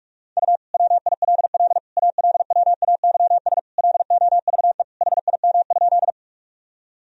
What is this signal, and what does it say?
Morse code, heard as a keyed tone, about 41 wpm, 700 Hz. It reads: U WILL ALWAYS LOVE HIM?
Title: Sentences from Top 400 Words and Abbreviations 45wpm